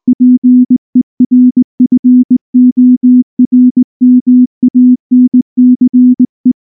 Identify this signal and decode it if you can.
Morse code, heard as a keyed tone, about 19 wpm, 250 Hz. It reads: PERFORMANCE
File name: morse / morse-training-20WPM-1